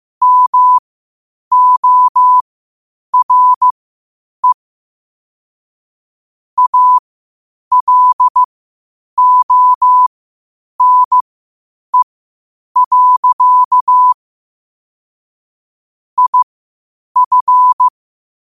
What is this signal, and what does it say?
Morse code, heard as a keyed tone, about 15 wpm, spaced as for 8 wpm, 1000 Hz. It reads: MORE ALONE. IF
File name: qotdfiles / 8wpm